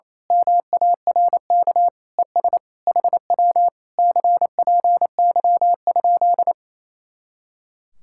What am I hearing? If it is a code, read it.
Morse code, heard as a keyed tone, about 27 wpm, 700 Hz. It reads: MARK ES HW CPY?
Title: Speed-Racing Common QSO Phrases Encore - 28wpm